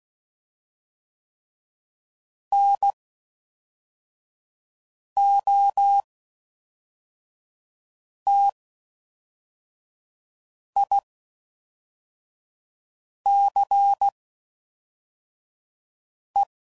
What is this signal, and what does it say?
Morse code, heard as a keyed tone, about 16 wpm, spaced as for 4 wpm, 800 Hz. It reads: NOTICE